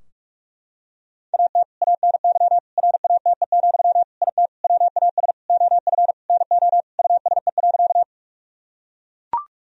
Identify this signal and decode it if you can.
Morse code, heard as a keyed tone, about 41 wpm, 700 Hz. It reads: AT ANY RATE, IT WAS OF NO USE.
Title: Sentences from Top 1000 Words - Review Part 4 - 45wpm